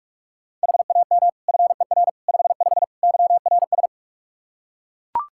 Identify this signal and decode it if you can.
Morse code, heard as a keyed tone, about 41 wpm, 700 Hz. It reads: HAM FER 55 YRS
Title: K1USN SST QSO Practice 45wpm